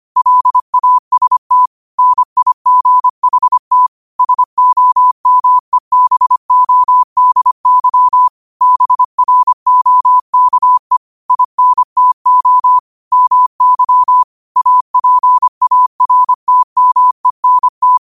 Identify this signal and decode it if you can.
Morse code, heard as a keyed tone, about 25 wpm, 1000 Hz. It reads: LAST NIGHT SOMEBODY BROKE INTO MY APARTMENT
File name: qotdfiles / 25wpm